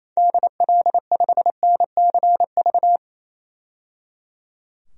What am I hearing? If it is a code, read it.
Morse code, heard as a keyed tone, about 28 wpm, 700 Hz. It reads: DL5NCV
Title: DXCC-100 Call Signs - 6 Characters - 28wpm